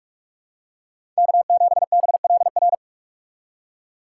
Morse code, heard as a keyed tone, about 42 wpm, 700 Hz. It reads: K7BLR
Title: USA Call Signs - 5 Characters - 4x Character Spacing - 45wpm